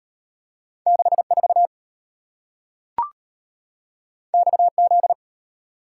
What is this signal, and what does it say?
Morse code, heard as a keyed tone, about 33 wpm, 690 Hz. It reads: 64 XZ